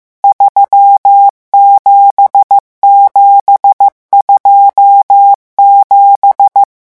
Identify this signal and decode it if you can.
Morse code, heard as a keyed tone, about 15 wpm, 800 Hz. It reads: 37727
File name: morse mp3 / revision11 chiffres vitesse 14